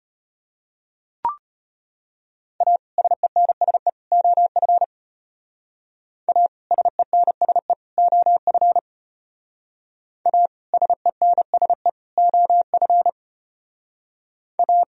Morse code, heard as a keyed tone, about 30 wpm, 700 Hz. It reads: A SENSE OF A SENSE OF A SENSE OF A
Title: Speed-Racing - Sets of 3 Words 25wpm